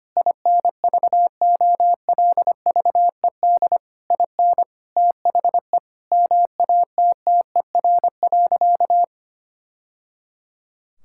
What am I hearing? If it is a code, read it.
Morse code, heard as a keyed tone, about 25 wpm, 700 Hz. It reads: INVOLVED IN THE MATTER.
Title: Sentences from Top 1000 Words - Review Part 5 - 25wpm